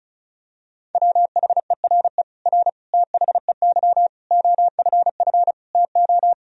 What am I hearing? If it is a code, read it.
Morse code, heard as a keyed tone, about 36 wpm, 700 Hz. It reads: WHERE R THEY OFF TO
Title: Sentences from Top 200 Words and Abbreviations 35wpm